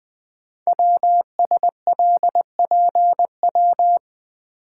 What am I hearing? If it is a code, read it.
Morse code, heard as a keyed tone, about 20 wpm, 700 Hz. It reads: WSLPW
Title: Code Groups - 5 Letters - 20wpm